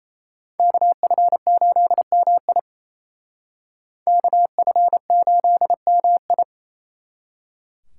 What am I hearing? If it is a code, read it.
Morse code, heard as a keyed tone, about 36 wpm, 700 Hz. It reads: KF8MI KF8MI